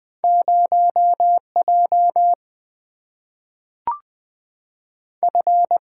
Morse code, heard as a keyed tone, about 20 wpm, 700 Hz. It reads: 0 J F
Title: Single-Letter Single-Number 20wpm